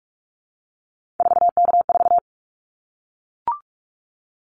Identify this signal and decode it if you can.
Morse code, heard as a keyed tone, about 44 wpm, 700 Hz. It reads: 4K4
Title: Callsign Prefixes for Top 100 DXCC - 45wpm